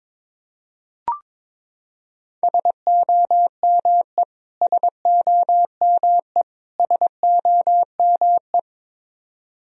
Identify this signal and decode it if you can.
Morse code, heard as a keyed tone, about 22 wpm, 700 Hz. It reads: SOME SOME SOME
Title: Top 100 Words Repeated 22wpm